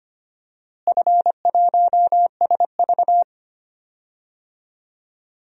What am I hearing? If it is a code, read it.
Morse code, heard as a keyed tone, about 25 wpm, 700 Hz. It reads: F1SV
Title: DXCC-100 Call Signs - 4 Characters - Repeated 2x - 25wpm